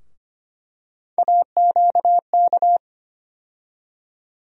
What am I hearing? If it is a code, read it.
Morse code, heard as a keyed tone, about 25 wpm, 700 Hz. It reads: AQK
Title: USA Call Sign Postfixes - 3 Letters 25wpm